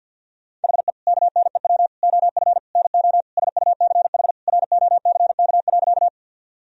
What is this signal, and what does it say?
Morse code, heard as a keyed tone, about 51 wpm, 700 Hz. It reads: HE KNEW OF NO SUCH ROCK.
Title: Sentences from Top 1000 Words - Review Part 2 - 50wpm